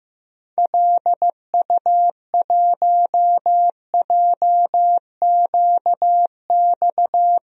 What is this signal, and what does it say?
Morse code, heard as a keyed tone, about 16 wpm, 700 Hz. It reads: LU1JQX
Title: DXCC-100 Call Signs - 6 Characters - 15wpm